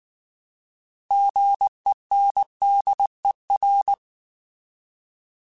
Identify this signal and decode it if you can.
Morse code, heard as a keyed tone, about 19 wpm, 800 Hz. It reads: GENDER